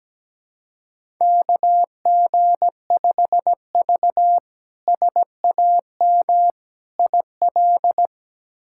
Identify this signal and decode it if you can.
Morse code, heard as a keyed tone, about 17 wpm, 700 Hz. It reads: KG5V SAM IL
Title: K1USN SST QSO Practice - 17wpm